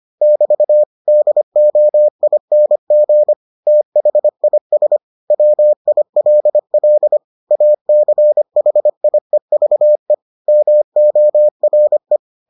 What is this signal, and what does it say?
Morse code, heard as a keyed tone, about 25 wpm, 600 Hz. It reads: <BT> DOING THIS WILL ACHIEVE MORE